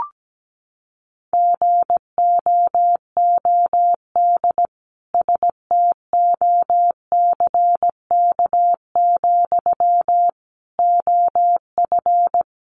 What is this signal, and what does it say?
Morse code, heard as a keyed tone, about 17 wpm, 700 Hz. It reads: GOOD STOCK, OF
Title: Sentences from Top 800 Words 17wpm